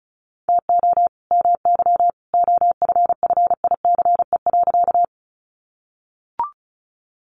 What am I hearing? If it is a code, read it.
Morse code, heard as a keyed tone, about 34 wpm, 700 Hz. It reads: TO MY OFFICE.
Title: Sentences from Top 1000 Words 35wpm